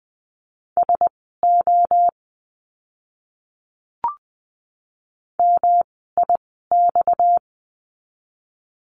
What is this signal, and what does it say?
Morse code, heard as a keyed tone, about 20 wpm, spaced as for 15 wpm, 700 Hz. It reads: SO MIX
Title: 230 Lesson 37f - Letter x - Words 20wpm